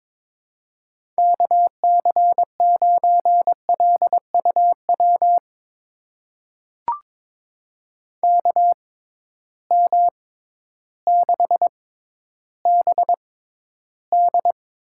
Morse code, heard as a keyed tone, about 23 wpm, spaced as for 12 wpm, 700 Hz. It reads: KC9LUW KM6BD
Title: USA Call Signs - 6 Characters - 6x Character Spacing - 22wpm